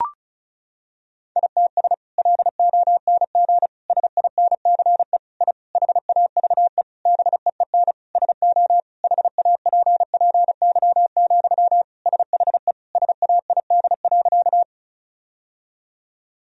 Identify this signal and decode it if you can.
Morse code, heard as a keyed tone, about 34 wpm, 700 Hz. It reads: ITS LONG SINCE I HAVE BEEN SO HAPPY, SHE SAID.
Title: Sentences from Top 1000 Words - Review Part 1 - 35wpm